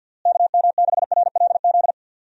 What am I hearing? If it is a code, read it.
Morse code, heard as a keyed tone, about 50 wpm, 700 Hz. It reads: KM6RLZ
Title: USA Call Signs - 6 Characters - Repeated 3x - 50wpm